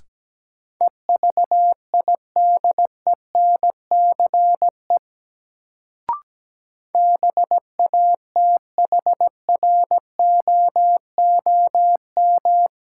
Morse code, heard as a keyed tone, about 17 wpm, 700 Hz. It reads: EVIDENCE BATHROOM
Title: Sets of 5 Words Encore - New Word Drill - 17wpm